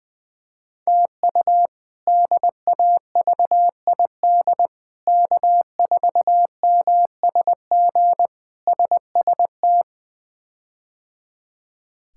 Morse code, heard as a keyed tone, about 20 wpm, 700 Hz. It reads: TU DAVID K4MSG SST
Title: K1USN SST QSO Practice - 20wpm